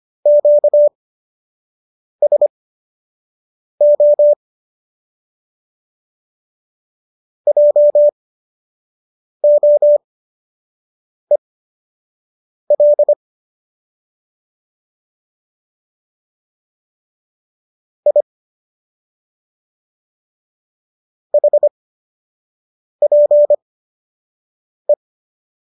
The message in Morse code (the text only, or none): QSO JOEL I HPE